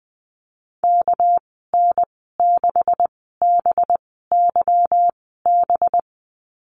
Morse code, heard as a keyed tone, about 20 wpm, spaced as for 14 wpm, 700 Hz. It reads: KN6BYB
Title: 213 Lesson 34g - Number 8 - Call Signs 20wpm